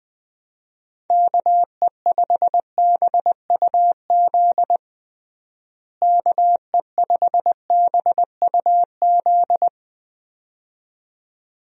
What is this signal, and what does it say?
Morse code, heard as a keyed tone, about 20 wpm, 700 Hz. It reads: KE5BUZ KE5BUZ
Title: USA Call Signs - 6 Characters - Repeated 2x - 20wpm